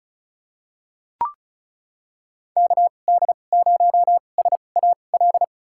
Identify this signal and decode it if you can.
Morse code, heard as a keyed tone, about 34 wpm, spaced as for 25 wpm, 700 Hz. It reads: KD0SAL
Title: USA Call Signs - 6 Characters - 2x Character Spacing - 35wpm